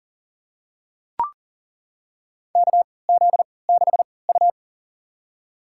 Morse code, heard as a keyed tone, about 40 wpm, spaced as for 23 wpm, 700 Hz. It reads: KZ6U
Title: USA Call Signs - 4 Characters - 3x Character Spacing - 40wpm